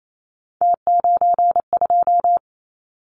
Morse code, heard as a keyed tone, about 28 wpm, 700 Hz. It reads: T92
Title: Callsign Prefixes for Top 100 DXCC - 28wpm